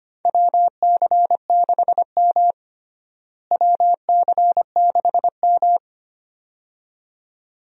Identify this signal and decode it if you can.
Morse code, heard as a keyed tone, about 25 wpm, 700 Hz. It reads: WC6M WC6M
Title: USA Call Signs - 4 Characters - Repeated 2x - 25wpm